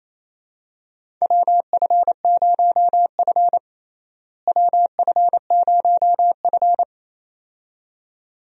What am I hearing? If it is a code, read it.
Morse code, heard as a keyed tone, about 28 wpm, 700 Hz. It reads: WF0F WF0F
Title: USA Call Signs - 4 Characters - Repeated 2x - 28wpm